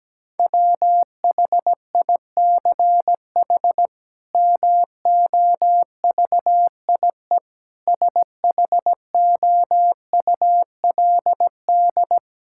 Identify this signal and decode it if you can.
Morse code, heard as a keyed tone, about 17 wpm, 700 Hz. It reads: WHICH MOVIE SHOULD
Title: Sentences from Top 1000 Words - Review Part 5 - 17wpm